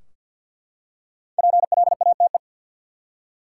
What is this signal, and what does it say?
Morse code, heard as a keyed tone, about 50 wpm, 700 Hz. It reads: PLATE
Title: Five Letter Words - 50wpm